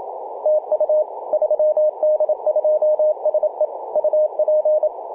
Morse code, heard as a keyed tone, about 27 wpm, 610 Hz, with radio noise. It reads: TU 3D2SE UP
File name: 3d2se-18-080mhz-cw